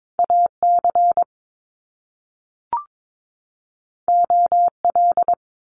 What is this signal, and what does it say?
Morse code, heard as a keyed tone, about 22 wpm, 700 Hz. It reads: AC OL
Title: Top Two Letter Combinations in Words 22wpm